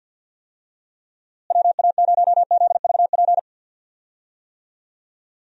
Morse code, heard as a keyed tone, about 46 wpm, 700 Hz. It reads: WA0ZVP
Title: USA Call Signs - 6 Characters - Repeated 2x - 50wpm